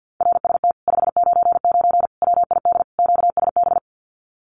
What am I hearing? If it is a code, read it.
Morse code, heard as a keyed tone, about 50 wpm, 700 Hz. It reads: RST 599 WID QSB
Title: Common QSO Phrases Encore 50wpm